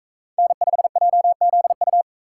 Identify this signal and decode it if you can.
Morse code, heard as a keyed tone, about 44 wpm, 700 Hz. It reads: N5JZU